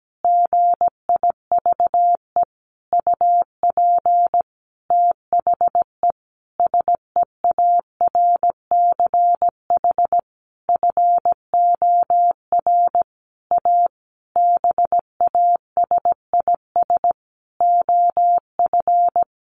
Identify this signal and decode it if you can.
Morse code, heard as a keyed tone, about 17 wpm, 700 Hz. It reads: GIVE UP THE SEARCH FOR A BASIS OF
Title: Sentences from Top 1500 Words 17wpm